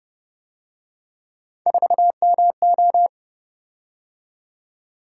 Morse code, heard as a keyed tone, about 29 wpm, 700 Hz. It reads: 4MO